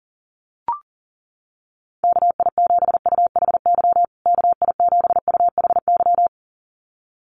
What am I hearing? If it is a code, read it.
Morse code, heard as a keyed tone, about 40 wpm, 700 Hz. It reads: KI7UHY KI7UHY
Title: USA Call Signs Repeated 2x 40wpm